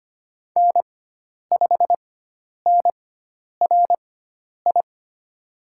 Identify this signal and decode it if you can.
Morse code, heard as a keyed tone, about 25 wpm, spaced as for 10 wpm, 700 Hz. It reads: N5NRI
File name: Lesson 16g - Letter c - Call Signs 25wpm